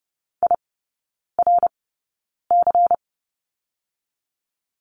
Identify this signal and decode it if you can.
Morse code, heard as a keyed tone, about 30 wpm, spaced as for 9 wpm, 690 Hz. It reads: IRC
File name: Lesson 17e - Letter u - Three Characters 30wpm